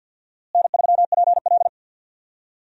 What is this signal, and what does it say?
Morse code, heard as a keyed tone, about 50 wpm, 700 Hz. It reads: N3PL
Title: USA Call Signs - 4 Characters - Repeated 3x - 50wpm